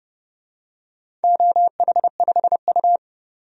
Morse code, heard as a keyed tone, about 30 wpm, 700 Hz. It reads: OH5U
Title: DXCC-100 Call Signs - 4 Characters - 30wpm